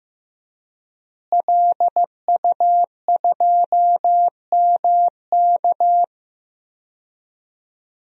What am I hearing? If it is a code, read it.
Morse code, heard as a keyed tone, about 15 wpm, 700 Hz. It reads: LU2MK
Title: DXCC-100 Call Signs - 5 Characters - Repeated 3x - 15wpm